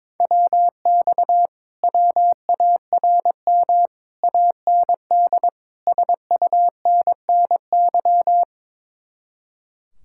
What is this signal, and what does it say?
Morse code, heard as a keyed tone, about 22 wpm, 700 Hz. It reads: WX WARM AND SUNNY
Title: Speed-Racing Common QSO Phrases - 22wpm